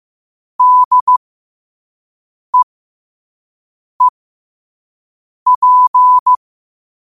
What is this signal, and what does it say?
Morse code, heard as a keyed tone, about 15 wpm, spaced as for 5 wpm, 1000 Hz. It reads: DEEP
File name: qotdfiles / 5wpm